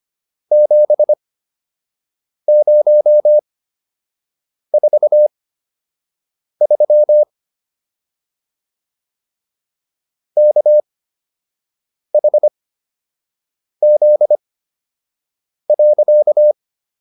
Morse code, heard as a keyed tone, about 25 wpm, spaced as for 6 wpm, 600 Hz. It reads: 7043 KHZ.